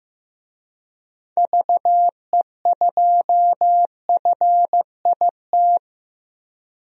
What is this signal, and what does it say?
Morse code, heard as a keyed tone, about 15 wpm, 700 Hz. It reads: VE2FIT
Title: DXCC-100 Call Signs - 6 Characters - 15wpm